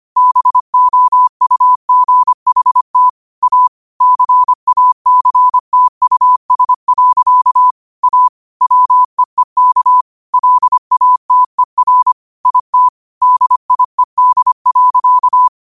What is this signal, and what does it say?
Morse code, heard as a keyed tone, about 25 wpm, 1000 Hz. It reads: BOUGHT A CACTUS. A WEEK LATER IT DIED.
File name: qotdfiles / 25wpm